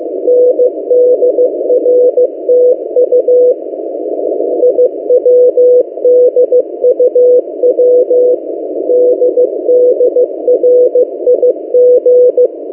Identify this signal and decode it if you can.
Morse code, heard as a keyed tone, about 15 wpm, 500 Hz. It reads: NDRTU IWDUW DDRIG
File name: M08a- -1355utc-8096khz